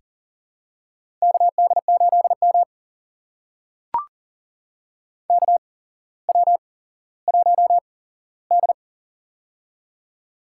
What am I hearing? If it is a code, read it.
Morse code, heard as a keyed tone, about 40 wpm, spaced as for 17 wpm, 700 Hz. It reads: KD8M KW1D
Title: USA Call Signs - 4 Characters - 8x Character Spacing - 40wpm